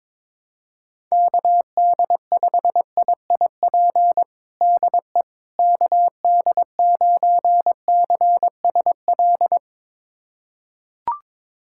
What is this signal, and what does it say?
Morse code, heard as a keyed tone, about 22 wpm, 700 Hz. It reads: KD5IIP DE KD9CSL
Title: USA Callsign DE Callsign - 6 Characters - 22wpm